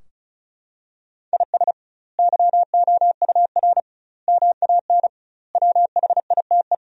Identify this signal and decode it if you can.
Morse code, heard as a keyed tone, about 36 wpm, 700 Hz. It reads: IS YOUR MAN WHITE
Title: 2x Word-Spacing - Sentences from Top 400 Words - 35wpm